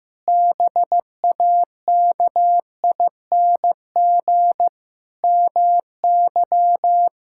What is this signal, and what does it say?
Morse code, heard as a keyed tone, about 15 wpm, 700 Hz. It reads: BAKING MY
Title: Pangram Challenge v2 - 15wpm